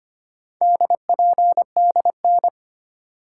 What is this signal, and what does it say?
Morse code, heard as a keyed tone, about 25 wpm, 700 Hz. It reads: DPDN
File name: practice / 25wpm / Four Letter-Number Combinations - 25wpm